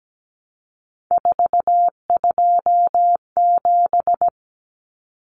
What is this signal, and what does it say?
Morse code, heard as a keyed tone, about 17 wpm, 700 Hz. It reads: 427
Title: Numbers - 3-Digit 17wpm